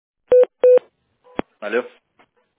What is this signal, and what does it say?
Speech. There is no code.